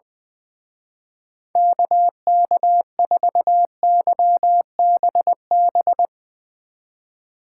Morse code, heard as a keyed tone, about 20 wpm, 700 Hz. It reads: KK4YBB
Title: Speed Racing - USA 2x3 Call Signs 20wpm